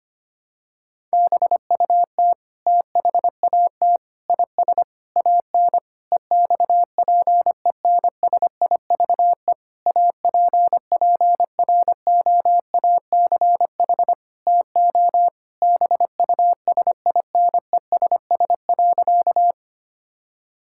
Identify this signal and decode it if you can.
Morse code, heard as a keyed tone, about 25 wpm, 700 Hz. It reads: BUT THAT IS AN EXPENSIVE APPROACH TO BUSINESS.
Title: Sentences from Top 1000 Words - Review Part 6 - 25wpm